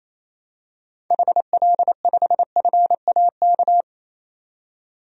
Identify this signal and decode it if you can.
Morse code, heard as a keyed tone, about 28 wpm, 700 Hz. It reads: HL5FAK
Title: DXCC-100 Call Signs - 6 Characters - 28wpm